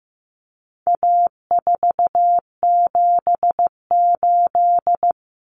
Morse code, heard as a keyed tone, about 15 wpm, 700 Hz. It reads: A478
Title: Common Cut Numbers - 4-Digit 15wpm